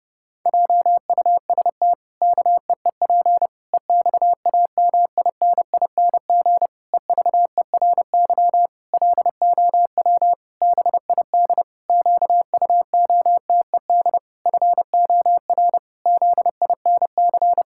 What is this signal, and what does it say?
Morse code, heard as a keyed tone, about 32 wpm, 700 Hz. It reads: JUST KEEP EXAMINING EVERY LOW BID QUOTED FOR ZINC
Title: Pangram Challenge v2 - 30wpm